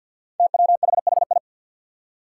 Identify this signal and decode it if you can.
Morse code, heard as a keyed tone, about 51 wpm, 700 Hz. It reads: TWHHI
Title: Five Challenging Letter-Number Combinations 50wpm